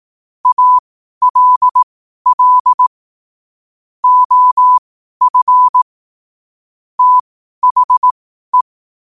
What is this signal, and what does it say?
Morse code, heard as a keyed tone, about 18 wpm, spaced as for 12 wpm, 1000 Hz. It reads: ALL OF THE
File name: qotdfiles / 12wpm